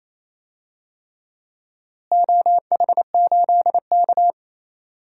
Morse code, heard as a keyed tone, about 27 wpm, 700 Hz. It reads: OH8K
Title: DXCC-100 Call Signs - 4 Characters - Repeated 3x - 28wpm